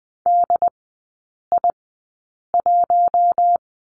Morse code, heard as a keyed tone, about 20 wpm, spaced as for 8 wpm, 700 Hz. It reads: DI1
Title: 067 Lesson 12e - Letter d - Three Characters 20wpm